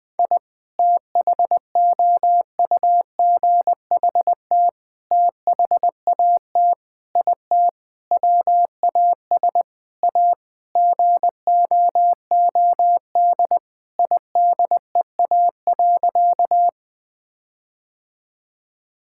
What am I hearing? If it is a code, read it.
Morse code, heard as a keyed tone, about 20 wpm, 700 Hz. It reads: I THOUGHT THAT IT WAS A GOOD IDEA.
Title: Sentences with 4-Word Phrases - Spoken Beginning - 20wpm